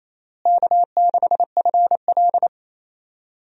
Morse code, heard as a keyed tone, about 28 wpm, 700 Hz. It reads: K6FL
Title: USA Call Signs - 4 Characters - Repeated 3x - 28wpm